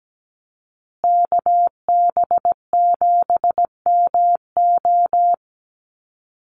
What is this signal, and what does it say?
Morse code, heard as a keyed tone, about 17 wpm, 700 Hz. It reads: KB7MO
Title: USA Call Signs Repeated 2x 17wpm